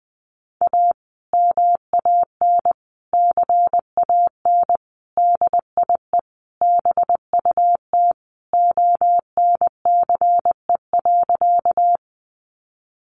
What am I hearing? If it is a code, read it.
Morse code, heard as a keyed tone, about 20 wpm, 700 Hz. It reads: A MAN CAN DIE BUT ONCE.
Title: Sentences from Top 1300 Words 20wpm